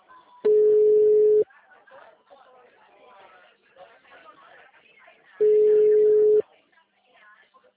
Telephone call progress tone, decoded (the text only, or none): ringback tone